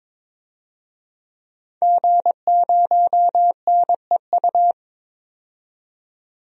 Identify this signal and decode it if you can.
Morse code, heard as a keyed tone, about 22 wpm, 700 Hz. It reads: G0NEU
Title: DXCC-100 Call Signs - 5 Characters - 22wpm